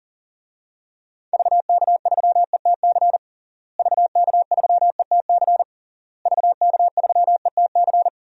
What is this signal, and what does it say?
Morse code, heard as a keyed tone, about 37 wpm, 700 Hz. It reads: VK3ETC VK3ETC VK3ETC
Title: DXCC-100 Call Signs - 6 Characters - Repeated 3x - 40wpm